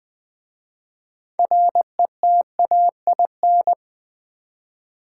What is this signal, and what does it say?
Morse code, heard as a keyed tone, about 20 wpm, 700 Hz. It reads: RETAIN
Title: 6 Letter Words - First 3 Letters Given - 20wpm